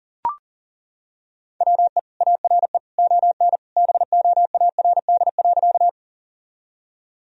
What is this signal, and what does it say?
Morse code, heard as a keyed tone, about 40 wpm, 700 Hz. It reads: WE ARE ON BOARD.
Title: Sentences from Top 1000 Words - Review Part 4 - 40wpm